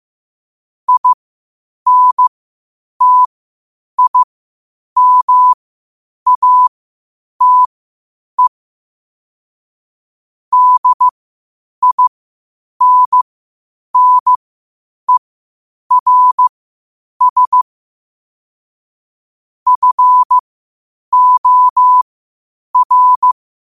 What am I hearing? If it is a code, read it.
Morse code, heard as a keyed tone, about 15 wpm, spaced as for 8 wpm, 1000 Hz. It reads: INTIMATE DINNERS FOR
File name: qotdfiles / 8wpm